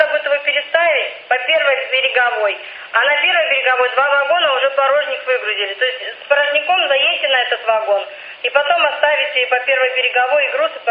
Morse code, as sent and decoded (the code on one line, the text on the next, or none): none